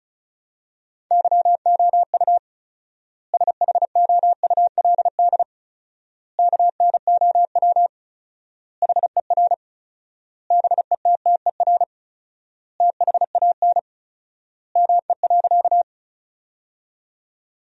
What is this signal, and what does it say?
Morse code, heard as a keyed tone, about 34 wpm, 700 Hz. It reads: YOU SHOULD KNOW HER BETTER THAN ME.
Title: 4x Word-Spacing - Sentences from Top 400 Words - 35wpm